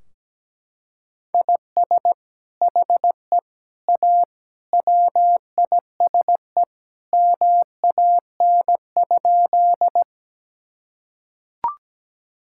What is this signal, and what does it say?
Morse code, heard as a keyed tone, about 17 wpm, 700 Hz. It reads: IS HE A WISE MAN?
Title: Sentences from Top 1600 Words 17wpm